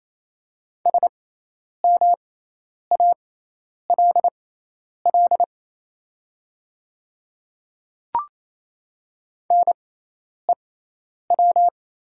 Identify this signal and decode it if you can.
Morse code, heard as a keyed tone, about 28 wpm, spaced as for 10 wpm, 690 Hz. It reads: SMALL NEW
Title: 6x Character Spacing - Top 300 Words - 28wpm